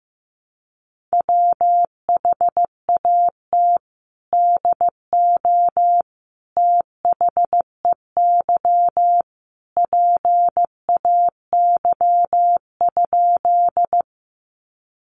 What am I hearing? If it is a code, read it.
Morse code, heard as a keyed tone, about 15 wpm, 700 Hz. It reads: WHAT DO THEY PAY?
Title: Sentences from Top 700 Words 15wpm